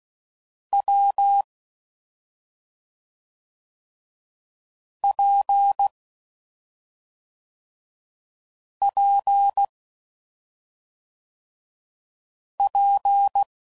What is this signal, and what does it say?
Morse code, heard as a keyed tone, about 16 wpm, 800 Hz. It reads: W P P P
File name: Combinations / Difficult Combos / WPJ WPJ1